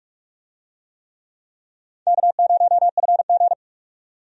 Morse code, heard as a keyed tone, about 45 wpm, 700 Hz. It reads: K0FG
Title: USA Call Signs - 4 Characters - 45wpm